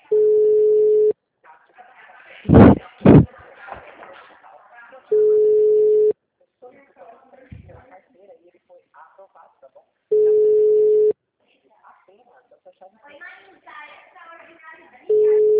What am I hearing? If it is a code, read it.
Ringback tone (Europe, 425 Hz), with speech.
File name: out- -2009- - - .18